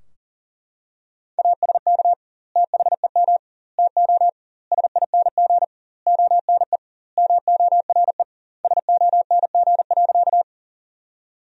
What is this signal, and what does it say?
Morse code, heard as a keyed tone, about 40 wpm, 700 Hz. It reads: ASK THEM TO SING ONE MORE SONG.
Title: 2x Word-Spacing - Sentences from Top 400 Words - 40wpm